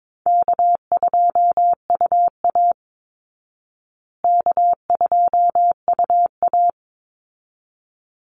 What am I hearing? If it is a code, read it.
Morse code, heard as a keyed tone, about 22 wpm, 700 Hz. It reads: K2UA K2UA
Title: Call Signs Repeated 3x - 22wpm